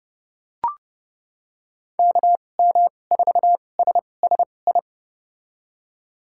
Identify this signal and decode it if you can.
Morse code, heard as a keyed tone, about 30 wpm, spaced as for 22 wpm, 700 Hz. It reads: KM4SSI
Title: USA Call Signs - 6 Characters - 2x Character Spacing - 30wpm